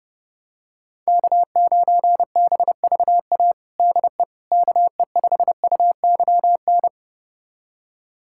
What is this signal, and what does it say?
Morse code, heard as a keyed tone, about 30 wpm, 700 Hz. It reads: K9BVA DE KE5UYN
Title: USA Callsign DE Callsign 30wpm